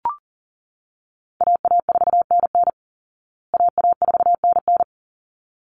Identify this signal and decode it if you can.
Morse code, heard as a keyed tone, about 40 wpm, 700 Hz. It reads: AA4NN AA4NN
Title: Call Signs Repeated 3x - 40wpm